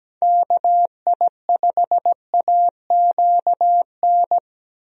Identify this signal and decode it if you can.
Morse code, heard as a keyed tone, about 17 wpm, 700 Hz. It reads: KI5AQN
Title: USA Call Signs - 6 Characters - Repeated 2x - 17wpm